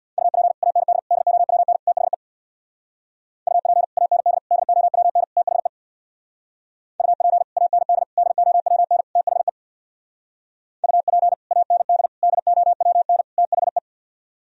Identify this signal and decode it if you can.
Morse code, heard as a keyed tone, about 64 wpm, 700 Hz. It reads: UP AND DOWN THE UP AND DOWN THE UP AND DOWN THE UP AND DOWN THE